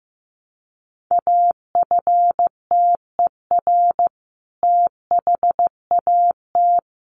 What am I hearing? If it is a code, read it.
Morse code, heard as a keyed tone, about 15 wpm, 700 Hz. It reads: AFTER THAT
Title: Sentences from Top 1100 Words 15wpm